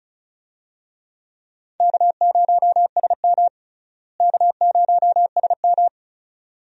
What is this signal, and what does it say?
Morse code, heard as a keyed tone, about 36 wpm, 700 Hz. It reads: K0SM K0SM
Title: USA Call Signs - 4 Characters - Repeated 2x - 35wpm